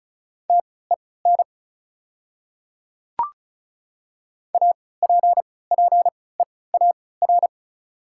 Morse code, heard as a keyed tone, about 36 wpm, spaced as for 20 wpm, 690 Hz. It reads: TEN APPEAR
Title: 3x Character Spacing - Top 500 Words - 35wpm